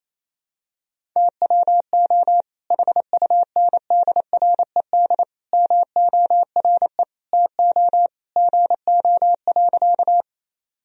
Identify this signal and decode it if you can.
Morse code, heard as a keyed tone, about 27 wpm, 700 Hz. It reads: TWO HUNDRED MORE TO GO.
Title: Sentences from Top 1000 Words - Review Part 2 - 28wpm